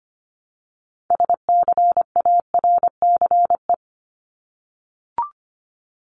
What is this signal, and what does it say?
Morse code, heard as a keyed tone, about 25 wpm, 700 Hz. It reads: SCARCE
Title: LY Suffix - Top 100 Words with Base Words 25wpm